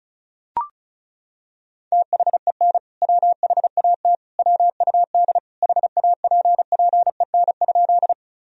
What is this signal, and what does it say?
Morse code, heard as a keyed tone, about 36 wpm, 700 Hz. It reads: THEN WHAT WUD HAPPEN?
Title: Sentences from Top 400 Words and Abbreviations 35wpm